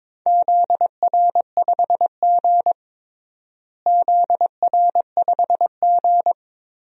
Morse code, heard as a keyed tone, about 22 wpm, 700 Hz. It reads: ZR5G ZR5G
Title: DXCC-100 Call Signs - 4 Characters - Repeated 3x - 22wpm